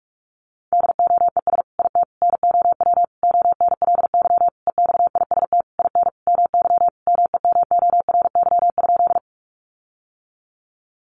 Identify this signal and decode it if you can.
Morse code, heard as a keyed tone, about 44 wpm, 700 Hz. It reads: DOES IT NOW ONLY EXIST IN MY MEMORY?
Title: Sentences from Top 1300 Words 45wpm